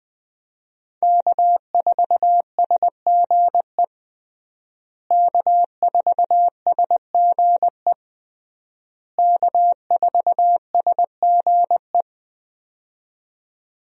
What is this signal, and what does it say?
Morse code, heard as a keyed tone, about 20 wpm, 700 Hz. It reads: K4SGE K4SGE K4SGE
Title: USA Call Signs - 5 Characters - Repeated 3x - 20wpm